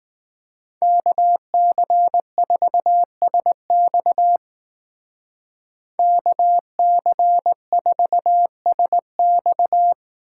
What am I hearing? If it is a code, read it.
Morse code, heard as a keyed tone, about 18 wpm, 700 Hz. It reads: KC4SX KC4SX
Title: Speed Racing - USA Call Signs 15wpm